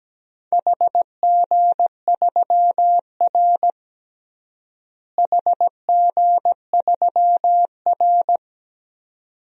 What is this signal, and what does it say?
Morse code, heard as a keyed tone, about 17 wpm, 700 Hz. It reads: HG3R HG3R
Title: DXCC-100 Call Signs - 4 Characters - Repeated 2x - 17wpm